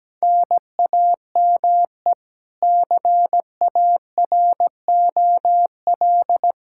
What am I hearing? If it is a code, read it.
Morse code, heard as a keyed tone, about 17 wpm, 700 Hz. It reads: NAME CAROL
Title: Practice QSO - Exchange 1 - Beginning - 17wpm